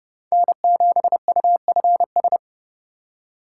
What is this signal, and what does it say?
Morse code, heard as a keyed tone, about 31 wpm, 690 Hz. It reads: N7UFS
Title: USA Call Signs - 5 Characters - 2x Character Spacing - 30wpm